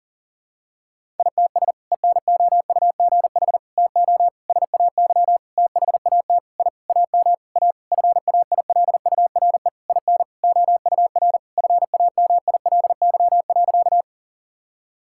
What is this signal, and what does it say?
Morse code, heard as a keyed tone, about 37 wpm, 700 Hz. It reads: ITS ENOUGH TO SAY THAT I AM A FAILURE IN OUR FAMILY.
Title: Sentences from Top 1000 Words - Review Part 6 - 40wpm